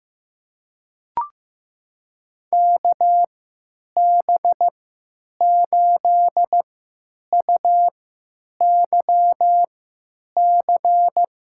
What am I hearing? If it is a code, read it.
Morse code, heard as a keyed tone, about 15 wpm, spaced as for 9 wpm, 700 Hz. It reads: KB8UYC
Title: USA Call Signs - 6 Characters - 3x Character Spacing - 15wpm